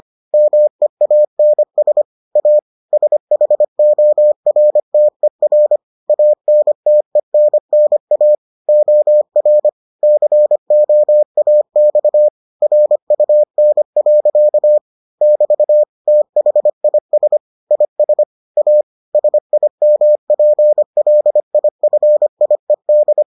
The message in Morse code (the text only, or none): MEANS A SHORTER ANTENNA OR COAX RUN. <BT> THIS IS A SIMPLIFIED